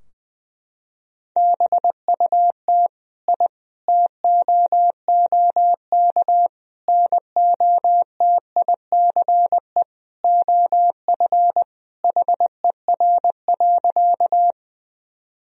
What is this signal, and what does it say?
Morse code, heard as a keyed tone, about 20 wpm, 700 Hz. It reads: BUT I TOOK NOTICE OF HER.
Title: Sentences from Top 1000 Words - Review Part 2 - 20wpm